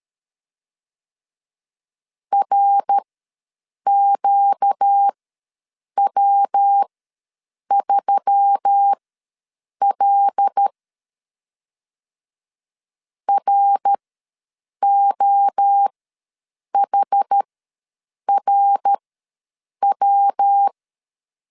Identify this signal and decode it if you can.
Morse code, heard as a keyed tone, about 13 wpm, spaced as for 7 wpm, 800 Hz. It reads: RQW3L ROHRW
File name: LEZIONI / 5 LBVC / 5 L ws 12wpm 2958